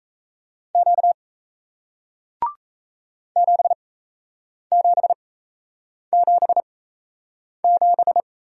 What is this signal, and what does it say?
Morse code, heard as a keyed tone, about 38 wpm, spaced as for 8 wpm, 700 Hz. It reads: Q 7777